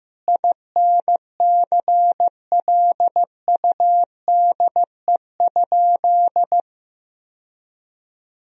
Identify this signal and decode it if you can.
Morse code, heard as a keyed tone, about 15 wpm, 700 Hz. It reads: INCLUDE?
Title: Sentences from Top 1000 Words - Review Part 3 - 15wpm